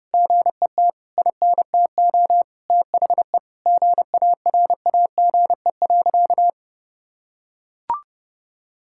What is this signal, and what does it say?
Morse code, heard as a keyed tone, about 30 wpm, 700 Hz. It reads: GET INTO THE GARAGE.
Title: Sentences from Top 1600 Words 30wpm